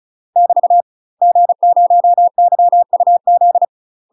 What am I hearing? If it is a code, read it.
Morse code, heard as a keyed tone, about 36 wpm, 700 Hz. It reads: <BT> G0YUZ
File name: UK MASTER QSOs TRAINING / 35WPM / UK MASTER QSOs 0026